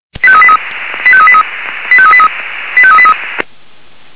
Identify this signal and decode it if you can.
Five-tone selective call (ZVEI), heard as quick runs of five tones, tones of about 70 ms. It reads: 85383 85383 85383 85383